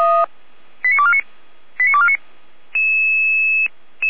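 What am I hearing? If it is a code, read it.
Five-tone selective call (ZVEI), heard as quick runs of five tones, tones of about 70 ms. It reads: 89249 89249
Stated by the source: ZVEI five-tone selective call received by SDR, Feuerwehr Orlamünde (f85360).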